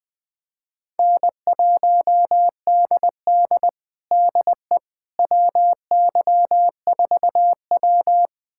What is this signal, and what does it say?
Morse code, heard as a keyed tone, about 20 wpm, 700 Hz. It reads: N1DD DE WY4W
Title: USA Callsign DE Callsign - 4 Characters - 20wpm